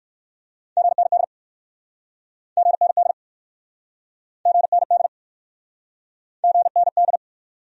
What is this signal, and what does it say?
Morse code, heard as a keyed tone, about 53 wpm, 700 Hz. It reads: GND GND GND GND
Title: Speed-Racing Abbreviations - 45wpm